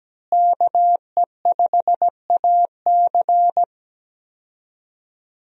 Morse code, heard as a keyed tone, about 17 wpm, 700 Hz. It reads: KE5AC
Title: USA Call Signs - 5 Characters - Repeated 2x - 17wpm